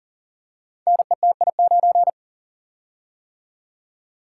Morse code, heard as a keyed tone, about 38 wpm, 700 Hz. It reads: NETI9
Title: Code Groups - 5 Letter:Numbers - 40wpm